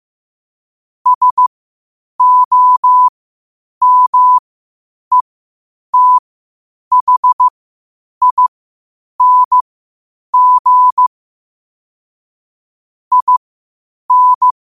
Morse code, heard as a keyed tone, about 15 wpm, spaced as for 8 wpm, 1000 Hz. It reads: SOMETHING IN
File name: qotdfiles / 8wpm